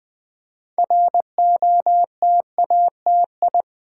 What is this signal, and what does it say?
Morse code, heard as a keyed tone, about 20 wpm, 700 Hz. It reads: ROTATI
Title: Build A Word - 3 Letters At A Time - 8-Letter Words - 20wpm